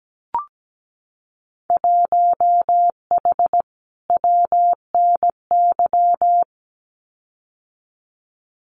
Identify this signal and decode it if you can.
Morse code, heard as a keyed tone, about 17 wpm, 700 Hz. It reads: 1H WNY
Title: Winter Field Day Exchanges 17wpm